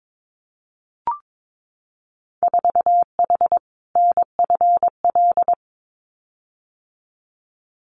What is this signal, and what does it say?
Morse code, heard as a keyed tone, about 22 wpm, 700 Hz. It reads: 4H NFL
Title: Winter Field Day Exchanges Encore 22wpm